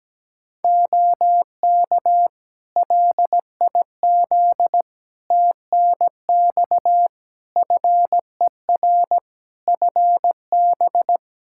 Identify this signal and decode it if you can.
Morse code, heard as a keyed tone, about 17 wpm, 700 Hz. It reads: OK LIZ TNX FER FB
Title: K1USN SST QSO Practice 17wpm